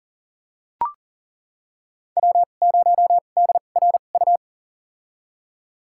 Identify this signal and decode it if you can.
Morse code, heard as a keyed tone, about 40 wpm, spaced as for 29 wpm, 700 Hz. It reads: W0DRU
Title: USA Call Signs - 5 Characters - 2x Character Spacing - 40wpm